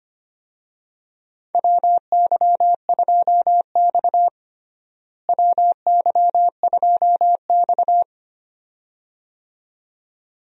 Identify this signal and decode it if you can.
Morse code, heard as a keyed tone, about 25 wpm, 700 Hz. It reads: WY2X WY2X
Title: USA Call Signs - 4 Characters - Repeated 2x - 25wpm